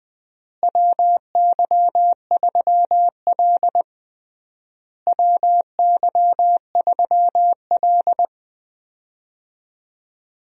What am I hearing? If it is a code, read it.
Morse code, heard as a keyed tone, about 20 wpm, 700 Hz. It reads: WY3L WY3L
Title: USA Call Signs - 4 Characters - Repeated 3x - 20wpm